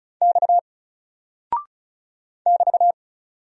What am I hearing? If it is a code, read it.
Morse code, heard as a keyed tone, about 34 wpm, 700 Hz. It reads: X <BT>
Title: Single Character (Letter:Number:Punctuation:Prosign) 35wpm